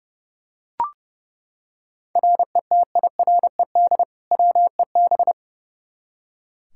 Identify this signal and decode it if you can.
Morse code, heard as a keyed tone, about 31 wpm, 700 Hz. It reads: RETIRED WEB
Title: Practice QSO - Exchange 3 - More Info - 30wpm